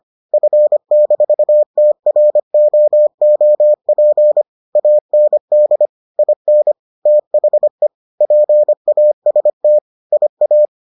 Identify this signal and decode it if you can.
Morse code, heard as a keyed tone, about 25 wpm, 600 Hz. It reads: F-TROOP AND IN THE PAST IA